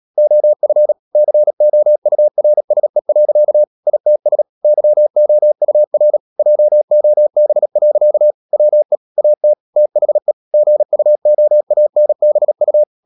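Morse code, heard as a keyed tone, about 37 wpm, 600 Hz. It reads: OF COURSE. ITS YOUR JOB. WE AT THE GUOANBU
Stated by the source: Morse code practice at 35–39 wpm sending English text of Ken Follett's novel Never, from CW med Gnister.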